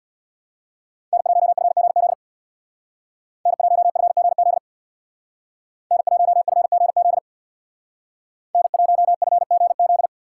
Morse code, heard as a keyed tone, about 60 wpm, 700 Hz. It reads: N1FGZ N1FGZ N1FGZ N1FGZ
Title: Speed Racing - USA Call Signs 50wpm